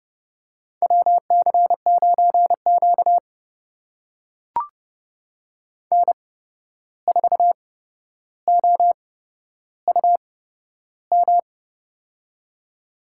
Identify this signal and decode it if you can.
Morse code, heard as a keyed tone, about 31 wpm, spaced as for 12 wpm, 700 Hz. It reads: WC9Q N4OUM
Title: USA Call Signs - 8x Character Spacing - 30wpm